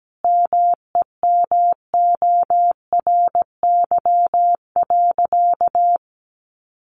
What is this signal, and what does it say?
Morse code, heard as a keyed tone, about 17 wpm, 700 Hz. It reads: MEMORY.
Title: Sentences from Top 1300 Words 17wpm